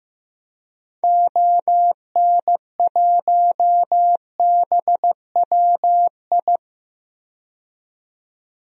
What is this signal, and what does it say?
Morse code, heard as a keyed tone, about 15 wpm, 700 Hz. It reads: ON1BWI
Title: DXCC-100 Call Signs - 6 Characters - 15wpm